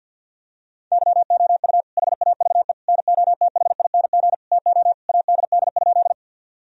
Morse code, heard as a keyed tone, about 52 wpm, 700 Hz. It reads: YOU HAVE NOTHING TO ADD?
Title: Sentences from Top 1000 Words - Review Part 1 - 50wpm